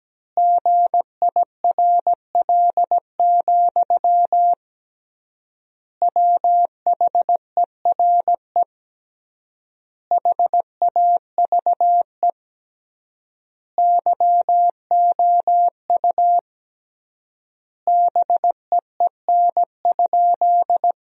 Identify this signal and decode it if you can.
Morse code, heard as a keyed tone, about 17 wpm, 700 Hz. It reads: GIRL, WHERE HAVE YOU BEEN?
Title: 3x Word-Spacing - Sentences from Top 400 Words - 17wpm